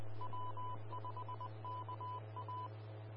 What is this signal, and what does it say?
Morse code, heard as a keyed tone, about 20 wpm, 1000 Hz. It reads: W5KA
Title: Austin, TX Amateur Repeaters